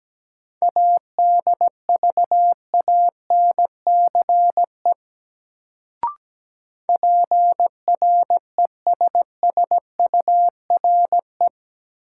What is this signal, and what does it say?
Morse code, heard as a keyed tone, about 17 wpm, 700 Hz. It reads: ADVANCE PRESSURE
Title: Sentences from Top 1400 Words - New Word Drill Alt - 17wpm